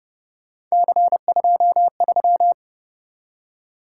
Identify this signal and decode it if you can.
Morse code, heard as a keyed tone, about 30 wpm, 700 Hz. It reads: C23